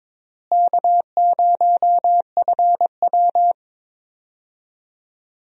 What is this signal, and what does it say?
Morse code, heard as a keyed tone, about 22 wpm, 700 Hz. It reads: K0FW
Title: USA Call Signs - 4 Characters - Repeated 3x - 22wpm